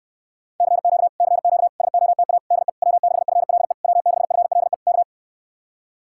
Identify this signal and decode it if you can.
Morse code, heard as a keyed tone, about 68 wpm, 700 Hz. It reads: CQ CQ SOTA DE W6FLE W6FLE K